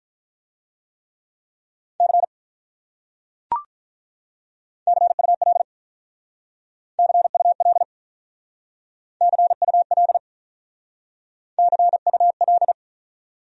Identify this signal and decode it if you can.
Morse code, heard as a keyed tone, about 44 wpm, 700 Hz. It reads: C CUL CUL CUL CUL